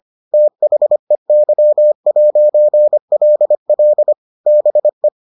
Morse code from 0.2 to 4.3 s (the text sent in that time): THEY'LL